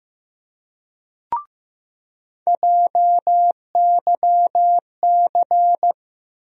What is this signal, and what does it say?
Morse code, heard as a keyed tone, about 15 wpm, 700 Hz. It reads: JYC
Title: USA Call Sign Postfixes - 3 Letters 15wpm